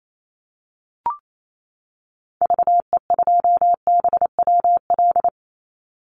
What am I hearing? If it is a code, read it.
Morse code, heard as a keyed tone, about 28 wpm, 700 Hz. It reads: VE2BWL
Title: Call Signs - 28wpm